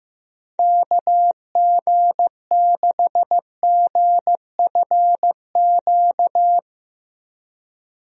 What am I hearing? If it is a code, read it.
Morse code, heard as a keyed tone, about 15 wpm, 700 Hz. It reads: KG6GFQ
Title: USA Call Signs - 6 Characters - 15wpm